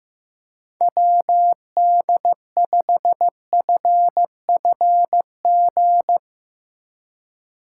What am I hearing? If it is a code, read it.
Morse code, heard as a keyed tone, about 15 wpm, 700 Hz. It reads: WD5FFG
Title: USA Call Signs - 6 Characters - 3x Character Spacing - 15wpm